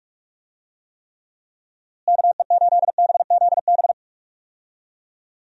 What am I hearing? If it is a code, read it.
Morse code, heard as a keyed tone, about 45 wpm, 700 Hz. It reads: KE8BZB
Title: USA Call Signs - 6 Characters - Repeated 3x - 45wpm